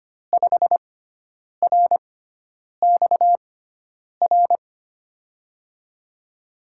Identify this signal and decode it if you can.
Morse code, heard as a keyed tone, about 25 wpm, spaced as for 9 wpm, 700 Hz. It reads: 5RXR
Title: USA Call Signs - 5 Characters - 6x Character Spacing - 25wpm